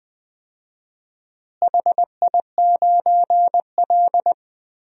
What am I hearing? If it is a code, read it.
Morse code, heard as a keyed tone, about 20 wpm, 700 Hz. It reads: HI9L
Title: DXCC-100 Call Signs - 4 Characters - Repeated 2x - 20wpm